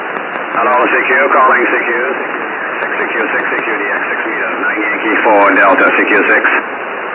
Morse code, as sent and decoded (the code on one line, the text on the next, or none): none